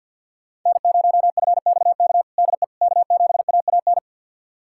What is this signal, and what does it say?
Morse code, heard as a keyed tone, about 51 wpm, 700 Hz. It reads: N0FXK DE K7AAN
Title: USA Callsign DE Callsign - 5 Characters - 50wpm